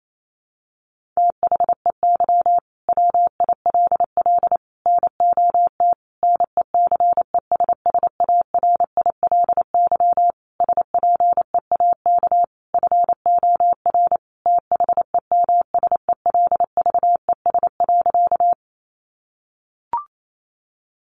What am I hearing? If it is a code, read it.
Morse code, heard as a keyed tone, about 28 wpm, 700 Hz. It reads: THEY WILL NOT NECESSARILY SPEAK FOR THEMSELVES.
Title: Sentences from Top 1400 Words 28wpm